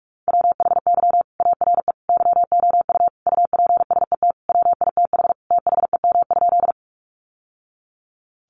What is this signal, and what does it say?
Morse code, heard as a keyed tone, about 44 wpm, 700 Hz. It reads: WHY ARE YOU UPSET WITH THEM?
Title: Sentences from Top 800 Words 45wpm